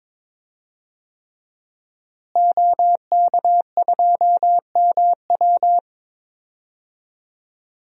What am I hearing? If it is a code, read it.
Morse code, heard as a keyed tone, about 22 wpm, 700 Hz. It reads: OK2MW
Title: DXCC-100 Call Signs - 5 Characters - Repeated 3x - 22wpm